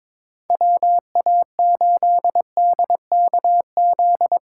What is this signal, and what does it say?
Morse code, heard as a keyed tone, about 22 wpm, 700 Hz. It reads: WA8DKZ
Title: USA Call Signs - 6 Characters - 4x Character Spacing - 22wpm